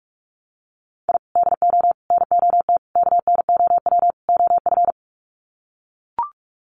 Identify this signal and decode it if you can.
Morse code, heard as a keyed tone, about 44 wpm, 700 Hz. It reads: I DO NOT KNOW OF
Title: Sets of 5 Words Encore2 45wpm